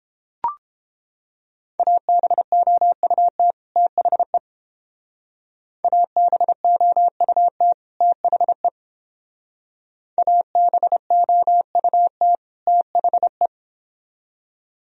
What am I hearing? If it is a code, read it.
Morse code, heard as a keyed tone, about 28 wpm, 700 Hz. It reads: ABOUT THE ABOUT THE ABOUT THE